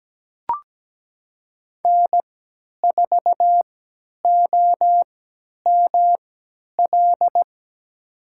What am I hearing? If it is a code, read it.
Morse code, heard as a keyed tone, about 17 wpm, spaced as for 10 wpm, 700 Hz. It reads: N4OML
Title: USA Call Signs - 3x Character Spacing - 17wpm